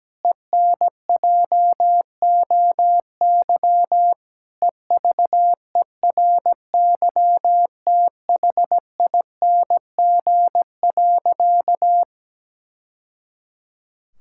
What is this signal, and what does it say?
Morse code, heard as a keyed tone, about 17 wpm, 700 Hz. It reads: ENJOY EVERYTHING.
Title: Sentences from Top 1000 Words - Review Part 5 - 17wpm